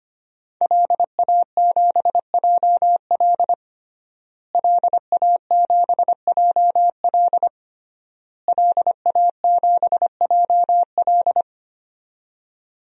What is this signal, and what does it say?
Morse code, heard as a keyed tone, about 25 wpm, 700 Hz. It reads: LA7JL LA7JL LA7JL
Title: DXCC-100 Call Signs - 5 Characters - Repeated 3x - 25wpm